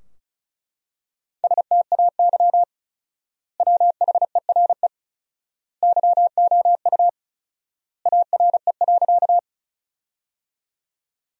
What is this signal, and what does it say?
Morse code, heard as a keyed tone, about 34 wpm, 700 Hz. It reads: STAY WHERE YOU ARE.
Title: 4x Word-Spacing - Sentences from Top 500 Words - 35wpm